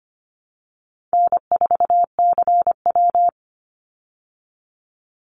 Morse code, heard as a keyed tone, about 25 wpm, 700 Hz. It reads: N4CW
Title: Easy Call Signs Repeated 2x - 25wpm